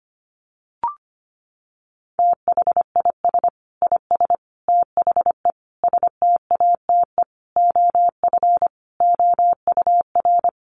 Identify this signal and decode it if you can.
Morse code, heard as a keyed tone, about 25 wpm, 700 Hz. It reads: THIS IS THE STATE OF OUR
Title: Sentences from Top 1400 Words 25wpm